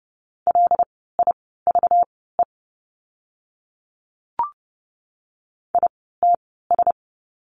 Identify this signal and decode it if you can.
Morse code, heard as a keyed tone, about 30 wpm, spaced as for 17 wpm, 700 Hz. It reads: LIVE ITS
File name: course / 30:10wpm / 237 Lesson 38f - Letter q - Words 30wpm